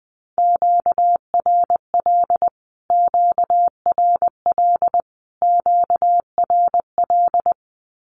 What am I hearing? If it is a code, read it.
Morse code, heard as a keyed tone, about 20 wpm, 700 Hz. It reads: QRL QRL QRL
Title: Q-Codes Repeated 3x 20wpm